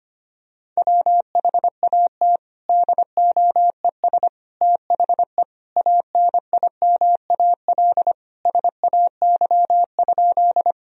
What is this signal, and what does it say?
Morse code, heard as a keyed tone, about 25 wpm, 700 Hz. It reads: WHAT DOES THE ANIMAL SAY?
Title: Sentences from Top 1000 Words - Review Part 1 - 25wpm